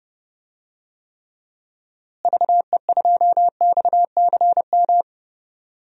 Morse code, heard as a keyed tone, about 30 wpm, 700 Hz. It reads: VE2XCM
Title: DXCC-100 Call Signs - 6 Characters - Repeated 2x - 30wpm